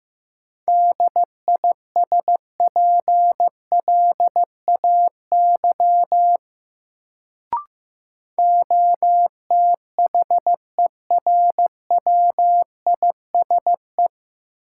Morse code, heard as a keyed tone, about 15 wpm, 700 Hz. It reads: DISPLAY OTHERWISE
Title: Sentences from Top 1200 Words - New Word Drill Alt - 15wpm